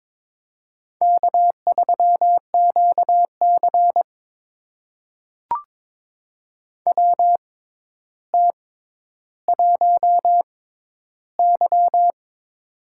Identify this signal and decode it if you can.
Morse code, heard as a keyed tone, about 22 wpm, spaced as for 11 wpm, 700 Hz. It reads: K3QC WT1Y
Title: USA Call Signs - 4 Characters - 6x Character Spacing - 22wpm